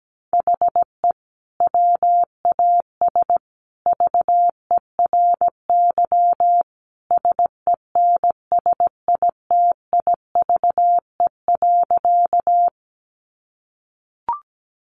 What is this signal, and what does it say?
Morse code, heard as a keyed tone, about 17 wpm, 700 Hz. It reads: HE WAS VERY SENSITIVE.
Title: Sentences from Top 1400 Words 17wpm